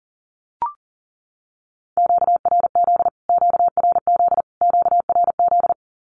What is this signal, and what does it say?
Morse code, heard as a keyed tone, about 40 wpm, 700 Hz. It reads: QRZ QRZ QRZ
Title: Q-Codes Repeated 3x 40wpm